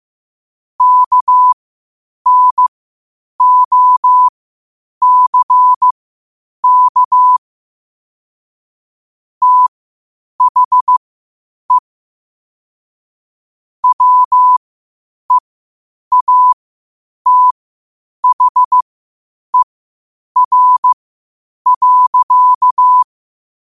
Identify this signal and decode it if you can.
Morse code, heard as a keyed tone, about 15 wpm, spaced as for 8 wpm, 1000 Hz. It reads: KNOCK THE WEATHER.
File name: qotdfiles / 8wpm